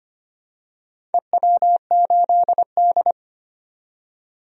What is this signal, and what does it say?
Morse code, heard as a keyed tone, about 25 wpm, 700 Hz. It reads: EW8D
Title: DXCC-100 Call Signs - 4 Characters - 25wpm